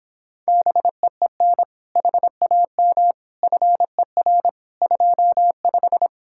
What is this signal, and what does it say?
Morse code, heard as a keyed tone, about 26 wpm, 700 Hz. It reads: BEEN HAM FER 25